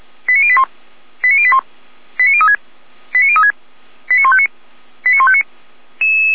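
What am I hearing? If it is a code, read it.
Five-tone selective call (ZVEI), heard as quick runs of five tones, tones of about 70 ms. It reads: 89081 89081 89036 89036 89159 89159
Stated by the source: ZVEI five-tone call 89084 received by SDR, fire brigade Feuerwehr Orlamünde.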